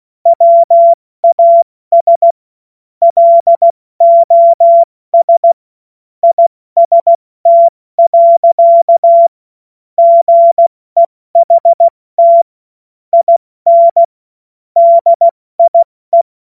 Morse code, heard as a keyed tone, about 16 wpm, 680 Hz. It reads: WAS LOS IST. GEHT IN DIE
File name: Morsekurs / Witze / Witz 09 April 2024 70 F80